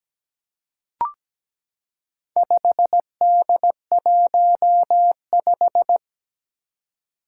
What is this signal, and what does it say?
Morse code, heard as a keyed tone, about 17 wpm, 700 Hz. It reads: 5D15
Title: Four Challenging Letter-Number Combinations 17wpm